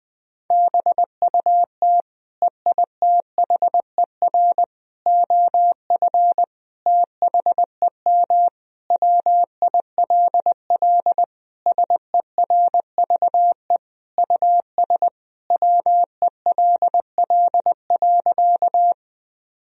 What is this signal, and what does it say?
Morse code, heard as a keyed tone, about 20 wpm, 700 Hz. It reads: BUT EITHER OF THEM WILL SERVE US WELL.
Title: Sentences from Top 1000 Words - Review Part 4 - 20wpm